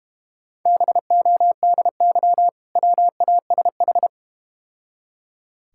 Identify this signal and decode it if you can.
Morse code, heard as a keyed tone, about 30 wpm, 690 Hz. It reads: BODY WASH
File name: Easy Words - Speed-Racing with Hints - 25wpm